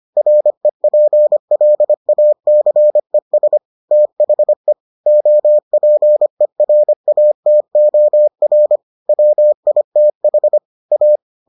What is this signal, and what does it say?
Morse code, heard as a keyed tone, about 25 wpm, 600 Hz. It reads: REPLACES THE OPERATOR WITH A